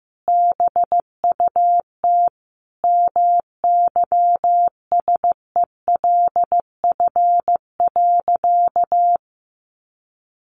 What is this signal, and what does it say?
Morse code, heard as a keyed tone, about 15 wpm, 700 Hz. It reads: BUT MYSELF.
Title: Sentences from Top 1300 Words 15wpm